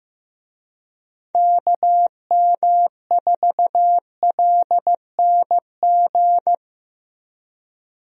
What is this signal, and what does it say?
Morse code, heard as a keyed tone, about 15 wpm, 700 Hz. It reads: KM4LNG
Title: USA Call Signs - 6 Characters - 15wpm